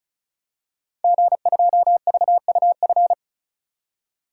Morse code, heard as a keyed tone, about 36 wpm, 690 Hz. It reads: G2VUF
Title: Code Groups - 5 Letter:Numbers - 35wpm